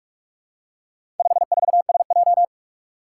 Morse code, heard as a keyed tone, about 44 wpm, 700 Hz. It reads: 54SJ
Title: Four Challenging Letter-Number Combinations 45wpm